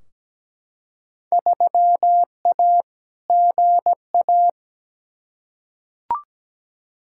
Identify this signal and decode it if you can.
Morse code, heard as a keyed tone, about 17 wpm, 700 Hz. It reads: 3A GA
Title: Field Day Exchanges - 17wpm